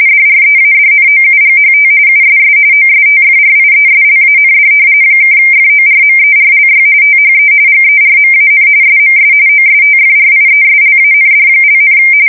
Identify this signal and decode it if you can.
RTTY, about 45.45 baud. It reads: DREADNOUGHT USS OSTFRIESLAND (FORMERLY THE GERMAN SMS OSTFRIESLAND),
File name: fsk170 / battleships